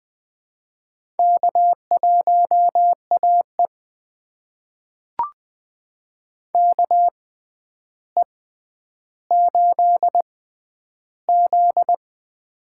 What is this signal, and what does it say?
Morse code, heard as a keyed tone, about 20 wpm, spaced as for 10 wpm, 700 Hz. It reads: K1AE KE8Z
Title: USA Call Signs - 4 Characters - 6x Character Spacing - 20wpm